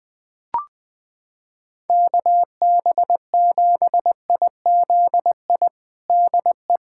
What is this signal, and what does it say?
Morse code, heard as a keyed tone, about 21 wpm, 700 Hz. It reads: KB7IZI DE
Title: USA Callsign DE Callsign - 6 Characters - 20wpm